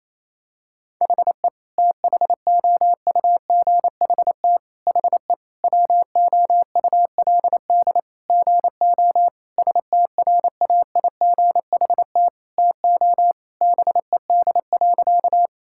Morse code, heard as a keyed tone, about 28 wpm, 700 Hz. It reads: HE THOUGHT HE WOULD GO STRAIGHT TO BED.
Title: Sentences from Top 1000 Words - Review Part 6 - 28wpm